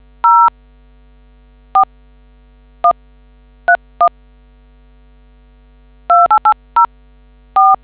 DTMF, heard as telephone touch tones, tones of about 100 ms. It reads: 04131 28804